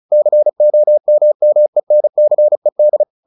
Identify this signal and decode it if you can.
Morse code, heard as a keyed tone, about 34 wpm, 600 Hz. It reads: COMMENCED
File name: the old man and the sea 2 35wpm